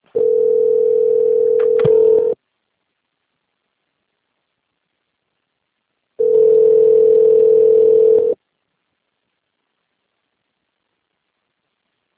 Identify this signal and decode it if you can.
Ringback tone (North America), 440 + 480 Hz.